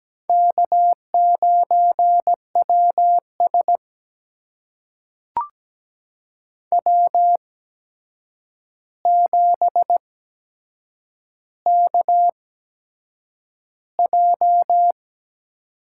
Morse code, heard as a keyed tone, about 17 wpm, spaced as for 7 wpm, 700 Hz. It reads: K9WS W7KJ
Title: USA Call Signs - 4 Characters - 8x Character Spacing - 17wpm